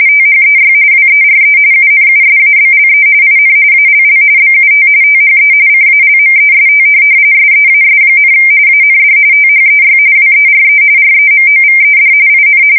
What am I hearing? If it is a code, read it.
RTTY, about 45.45 baud. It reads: THEY WERE UNUSABLE IN ALL BUT THE CALMEST SEAS, AND THEY WERE REMOVED IN